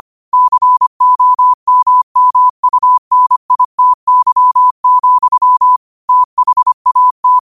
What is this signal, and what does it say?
Morse code, heard as a keyed tone, about 25 wpm, 1000 Hz. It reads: COMMUNITY, THAT